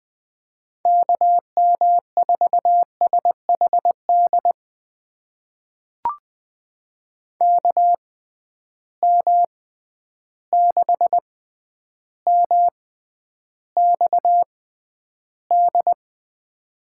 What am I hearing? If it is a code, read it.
Morse code, heard as a keyed tone, about 21 wpm, spaced as for 10 wpm, 700 Hz. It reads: KM4SHD KM6MXD
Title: USA Call Signs - 6 Characters - 6x Character Spacing - 20wpm